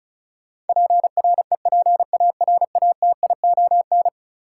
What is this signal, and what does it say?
Morse code, heard as a keyed tone, about 36 wpm, 700 Hz. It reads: PREPARATION
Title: Sentences from Top 1400 Words - New Word Drill Alt - 35wpm